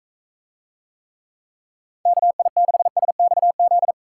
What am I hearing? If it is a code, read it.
Morse code, heard as a keyed tone, about 42 wpm, 700 Hz. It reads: KI6SXZ